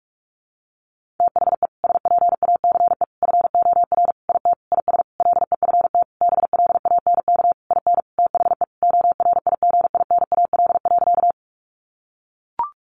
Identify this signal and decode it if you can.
Morse code, heard as a keyed tone, about 44 wpm, 700 Hz. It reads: THE SPACE FOR IT IS LEFT BLANK IN THE ORIGINAL.
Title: Sentences from Top 1500 Words 45wpm